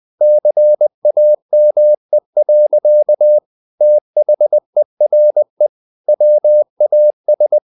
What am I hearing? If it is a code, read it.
Morse code, heard as a keyed tone, about 20 wpm, 600 Hz. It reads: CAME. THERE WAS
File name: the old man and the sea part 2 20wpm